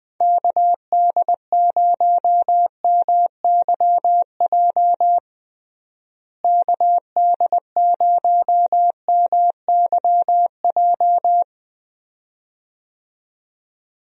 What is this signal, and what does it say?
Morse code, heard as a keyed tone, about 21 wpm, 700 Hz. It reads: KD0MYJ KD0MYJ
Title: USA Call Signs - 6 Characters - Repeated 2x - 20wpm